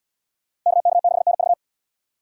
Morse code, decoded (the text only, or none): KK6NV